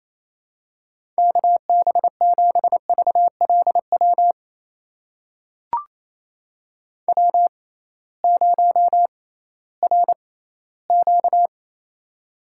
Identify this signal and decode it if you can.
Morse code, heard as a keyed tone, about 28 wpm, spaced as for 16 wpm, 700 Hz. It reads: KB7VLW W0RQ
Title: USA Call Signs - 6x Character Spacing - 28wpm